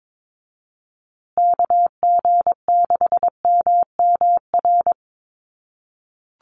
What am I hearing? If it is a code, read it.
Morse code, heard as a keyed tone, about 22 wpm, 700 Hz. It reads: KG6MMR
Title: USA Call Signs Repeated 3x 22wpm